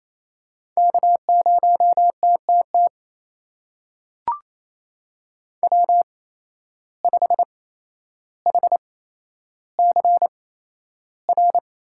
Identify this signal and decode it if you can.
Morse code, heard as a keyed tone, about 28 wpm, spaced as for 12 wpm, 700 Hz. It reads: K0TTT W5HCR
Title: USA Call Signs - 5 Characters - 8x Character Spacing - 28wpm